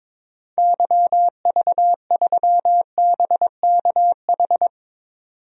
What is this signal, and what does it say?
Morse code, heard as a keyed tone, about 22 wpm, 700 Hz. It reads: YV3BKH
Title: DXCC-100 Call Signs - 6 Characters - 22wpm